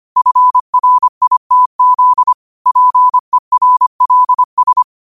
Morse code, heard as a keyed tone, about 25 wpm, 1000 Hz. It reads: FRITZ PERLS
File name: qotdfiles / 25wpm